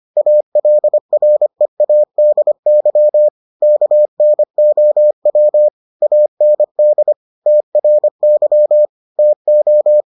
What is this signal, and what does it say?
Morse code, heard as a keyed tone, about 25 wpm, 600 Hz. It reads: ALREADY KNOW AND TRY TO